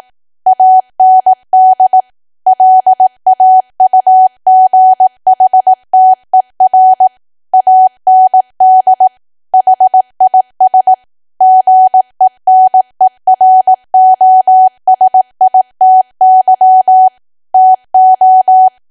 Morse code, heard as a keyed tone, about 18 wpm, 750 Hz. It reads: AND LAUGHTER AND HIS GENEROSITY TO
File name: Archive / 18 WPM / 18WPM